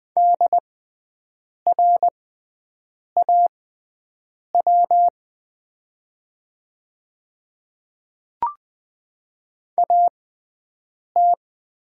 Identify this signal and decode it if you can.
Morse code, heard as a keyed tone, about 20 wpm, spaced as for 7 wpm, 700 Hz. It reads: DRAW AT
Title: 6x Character Spacing - Top 500 Words - 20wpm